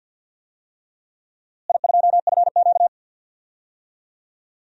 Morse code, heard as a keyed tone, about 48 wpm, 700 Hz. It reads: I2FQ
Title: DXCC-100 Call Signs - 4 Characters - Repeated 3x - 50wpm